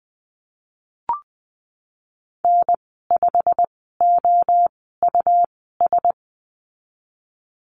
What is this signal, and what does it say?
Morse code, heard as a keyed tone, about 19 wpm, spaced as for 14 wpm, 700 Hz. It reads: N5OUS
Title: 122 Lesson 20g - Number 3 - Call Signs 20wpm